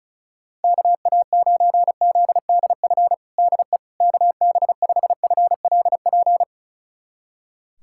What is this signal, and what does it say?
Morse code, heard as a keyed tone, about 36 wpm, 700 Hz. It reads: KA9ZDF DE KB5FLP
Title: USA Callsign DE Callsign - 6 Characters - 35wpm